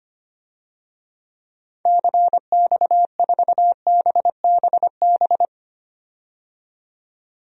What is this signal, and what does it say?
Morse code, heard as a keyed tone, about 25 wpm, 700 Hz. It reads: CX4BBB
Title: DXCC-100 Call Signs - 6 Characters - 25wpm